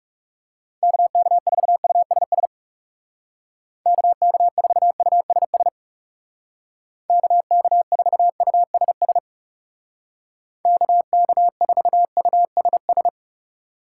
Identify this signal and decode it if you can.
Morse code, heard as a keyed tone, about 37 wpm, 700 Hz. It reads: KK4USS KK4USS KK4USS KK4USS
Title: Speed Racing - USA 2x3 Call Signs 30wpm